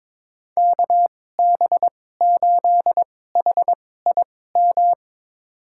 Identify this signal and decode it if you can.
Morse code, heard as a keyed tone, about 23 wpm, spaced as for 16 wpm, 700 Hz. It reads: KB8HIM
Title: USA Call Signs - 6 Characters - 2x Character Spacing - 22wpm